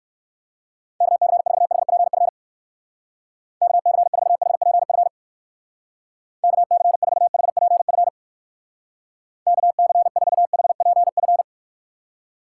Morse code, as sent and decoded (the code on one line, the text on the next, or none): -.- -.-. ....- .... .--. ..-. / -.- -.-. ....- .... .--. ..-. / -.- -.-. ....- .... .--. ..-. / -.- -.-. ....- .... .--. ..-.
KC4HPF KC4HPF KC4HPF KC4HPF